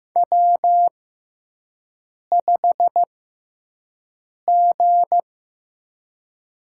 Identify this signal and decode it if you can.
Morse code, heard as a keyed tone, about 15 wpm, spaced as for 5 wpm, 700 Hz. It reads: W5G
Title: USA Call Signs - 5 Characters - 6x Character Spacing - 15wpm